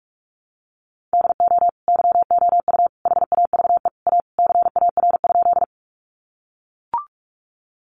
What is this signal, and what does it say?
Morse code, heard as a keyed tone, about 44 wpm, 700 Hz. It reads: DO YOU HAVE A CAR?
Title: Sentences from Top 300 Words 45wpm